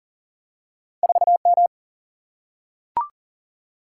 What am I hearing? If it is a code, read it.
Morse code, heard as a keyed tone, about 39 wpm, 700 Hz. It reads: 4M
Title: Single-Letter Single-Number 40wpm